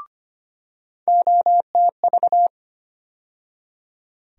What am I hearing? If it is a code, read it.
Morse code, heard as a keyed tone, about 25 wpm, 700 Hz. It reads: OTV
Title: USA Call Sign Postfixes - 3 Letters 25wpm